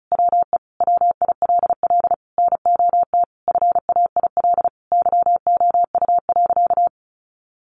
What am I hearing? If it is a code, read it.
Morse code, heard as a keyed tone, about 34 wpm, 700 Hz. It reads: WE WILL NOT FAIL YOU.
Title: Sentences from Top 1100 Words 35wpm